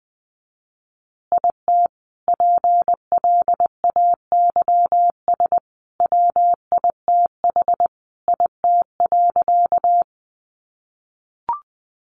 Morse code, heard as a keyed tone, about 20 wpm, 700 Hz. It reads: IT PLAYS WITH IT.
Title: Additional Easy Phrases 20wpm